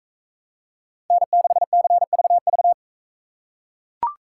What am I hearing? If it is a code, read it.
Morse code, heard as a keyed tone, about 41 wpm, 700 Hz. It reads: N6CVV